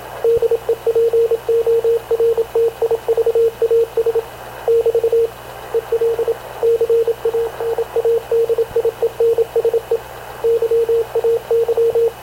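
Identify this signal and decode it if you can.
Morse code, heard as a keyed tone, about 26 wpm, 460 Hz, with radio noise. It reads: DEPORTIVAS <BT> EL CANADIENSE YAY
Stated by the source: Havana Radio CLA press (PX) broadcast, recorded by Radio Officer Paul Dumesnil aboard mv CNAV QUEST/CZDO.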